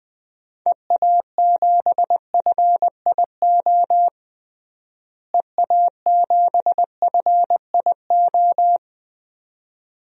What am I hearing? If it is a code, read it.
Morse code, heard as a keyed tone, about 20 wpm, 700 Hz. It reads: EA7FIO EA7FIO
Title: DXCC-100 Call Signs - 6 Characters - Repeated 2x - 20wpm